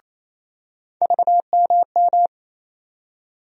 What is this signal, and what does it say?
Morse code, heard as a keyed tone, about 28 wpm, 700 Hz. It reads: VMM
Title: USA Call Sign Postfixes - 3 Letters 28wpm